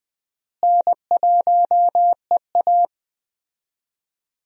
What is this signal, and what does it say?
Morse code, heard as a keyed tone, about 20 wpm, 700 Hz. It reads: N1EA
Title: USA Call Signs - 4 Characters - Repeated 2x - 20wpm